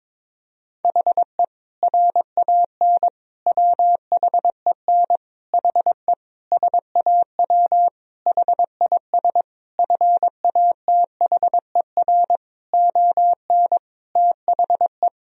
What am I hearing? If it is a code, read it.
Morse code, heard as a keyed tone, about 22 wpm, 700 Hz. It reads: HE RAN WHEN HE SAW HIS FATHER ON THE
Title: Sentences from Top 500 Words and Abbreviations 22wpm